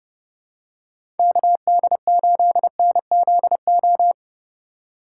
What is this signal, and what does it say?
Morse code, heard as a keyed tone, about 30 wpm, 700 Hz. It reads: KD8NZO
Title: USA Call Signs - 6 Characters - 4x Character Spacing - 30wpm